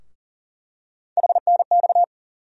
Morse code, heard as a keyed tone, about 40 wpm, 700 Hz. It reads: HNX
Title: USA Call Sign Postfixes - 3 Letters 40wpm